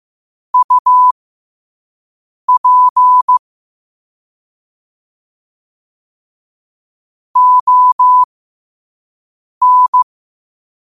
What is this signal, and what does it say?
Morse code, heard as a keyed tone, about 15 wpm, spaced as for 5 wpm, 1000 Hz. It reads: UP ON
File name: qotdfiles / 5wpm